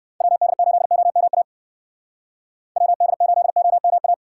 Morse code, heard as a keyed tone, about 58 wpm, 700 Hz. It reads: WD8OMA WD8OMA